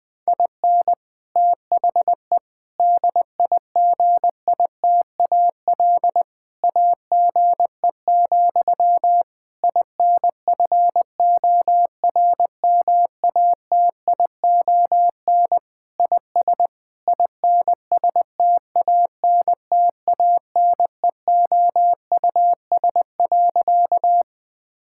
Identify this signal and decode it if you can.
Morse code, heard as a keyed tone, about 20 wpm, 700 Hz. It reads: IN THE DIGITAL AGE, INFORMATION IS INSTANTANEOUS.
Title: Difficult Sentences with Hints - 20wpm